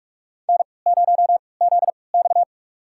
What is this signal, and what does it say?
Morse code, heard as a keyed tone, about 43 wpm, spaced as for 25 wpm, 700 Hz. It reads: N0ZX
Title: USA Call Signs - 4 Characters - 3x Character Spacing - 45wpm